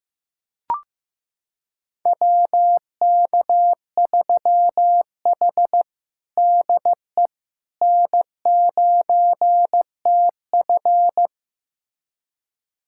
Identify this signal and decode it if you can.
Morse code, heard as a keyed tone, about 15 wpm, 700 Hz. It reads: WK3H DE N9TF
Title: USA Callsign DE Callsign - 4 Characters - 15wpm